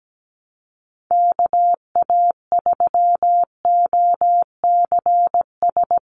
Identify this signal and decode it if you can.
Morse code, heard as a keyed tone, about 17 wpm, 700 Hz. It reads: KA3OCS
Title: Call Signs - 17wpm